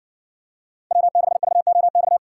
Morse code, heard as a keyed tone, about 59 wpm, 700 Hz. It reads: W6VOX